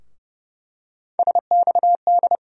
Morse code, heard as a keyed tone, about 31 wpm, 700 Hz. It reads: SXD
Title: USA Call Sign Postfixes - 3 Letters 30wpm